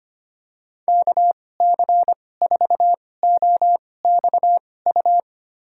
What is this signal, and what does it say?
Morse code, heard as a keyed tone, about 25 wpm, spaced as for 18 wpm, 700 Hz. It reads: KC4OXU
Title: USA Call Signs - 6 Characters - 2x Character Spacing - 25wpm